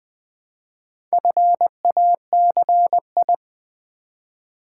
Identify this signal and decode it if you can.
Morse code, heard as a keyed tone, about 20 wpm, 700 Hz. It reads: FACI
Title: Build A Word - 2 Letters At A Time - 8-Letter Words - 20wpm